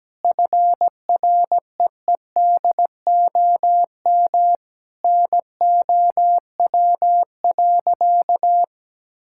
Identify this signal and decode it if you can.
Morse code, heard as a keyed tone, about 17 wpm, 700 Hz. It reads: FREEDOM NOW.
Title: Sentences from Top 1000 Words - Review Part 5 - 17wpm